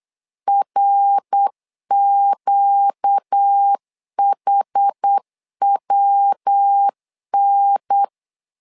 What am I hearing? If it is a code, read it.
Morse code, heard as a keyed tone, about 8 wpm, 800 Hz. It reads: RQHWN
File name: LEZIONI / 4 Q 8wpm 2732